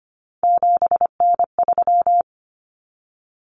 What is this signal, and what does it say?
Morse code, heard as a keyed tone, about 25 wpm, 700 Hz. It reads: 7N3
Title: Common Cut Numbers - 3-Digit 25wpm